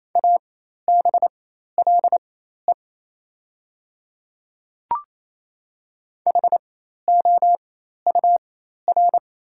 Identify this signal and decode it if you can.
Morse code, heard as a keyed tone, about 28 wpm, spaced as for 13 wpm, 700 Hz. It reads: ABLE HOUR
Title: 4x Character Spacing - Top 500 Words - 28wpm